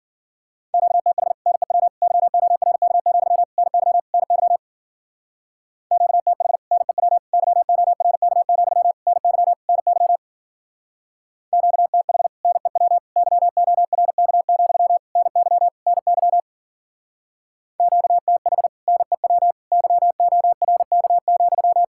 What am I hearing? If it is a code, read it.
Morse code, heard as a keyed tone, about 51 wpm, 700 Hz. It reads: QTH NEW YORK, NY NY QTH NEW YORK, NY NY QTH NEW YORK, NY NY QTH NEW YORK,